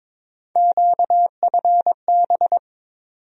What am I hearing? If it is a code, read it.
Morse code, heard as a keyed tone, about 22 wpm, 700 Hz. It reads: QFB
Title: USA Call Sign Postfixes - 3 Letters 22wpm